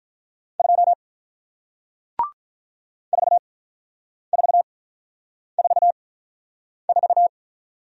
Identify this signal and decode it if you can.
Morse code, heard as a keyed tone, about 48 wpm, 700 Hz. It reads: 2 4 4 4 4